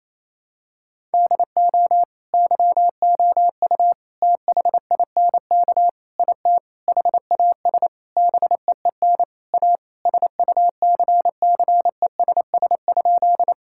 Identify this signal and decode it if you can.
Morse code, heard as a keyed tone, about 28 wpm, 700 Hz. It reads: DO YOU THINK IT HAS BEEN A SUCCESS?
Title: Sentences from Top 1000 Words - Review Part 6 - 28wpm